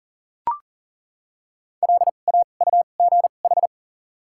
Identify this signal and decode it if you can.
Morse code, heard as a keyed tone, about 40 wpm, spaced as for 29 wpm, 700 Hz. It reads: LAUGH
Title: 2x Character Spacing - Top 500 Words - 40wpm